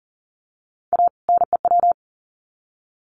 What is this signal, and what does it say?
Morse code, heard as a keyed tone, about 40 wpm, 700 Hz. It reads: A NEW
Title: Sets of 2 Words 40wpm